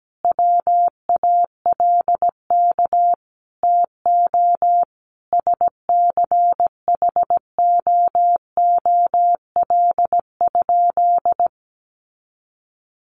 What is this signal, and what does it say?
Morse code, heard as a keyed tone, about 17 wpm, 700 Hz. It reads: WALK TO SCHOOL?
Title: Sentences from Top 1400 Words 17wpm